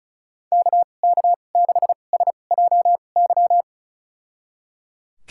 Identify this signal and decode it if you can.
Morse code, heard as a keyed tone, about 34 wpm, spaced as for 25 wpm, 700 Hz. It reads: KK6SJY
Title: USA Call Signs - 6 Characters - 2x Character Spacing - 35wpm